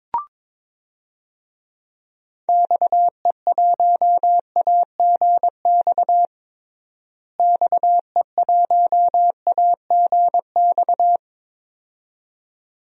Morse code, heard as a keyed tone, about 22 wpm, 700 Hz. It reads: XE1AGX XE1AGX
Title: DXCC-100 Call Signs - 6 Characters - Repeated 2x - 22wpm